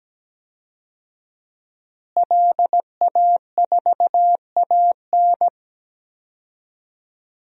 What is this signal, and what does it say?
Morse code, heard as a keyed tone, about 17 wpm, 700 Hz. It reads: LA4AN
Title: DXCC-100 Call Signs - 5 Characters - Repeated 2x - 17wpm